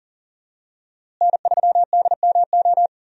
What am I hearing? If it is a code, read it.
Morse code, heard as a keyed tone, about 40 wpm, 700 Hz. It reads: N3DMO
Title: USA Call Signs - 5 Characters - Repeated 3x - 40wpm